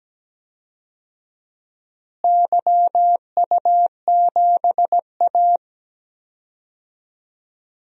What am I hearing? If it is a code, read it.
Morse code, heard as a keyed tone, about 17 wpm, 700 Hz. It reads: YU7A
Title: DXCC-100 Call Signs - 4 Characters - Repeated 2x - 17wpm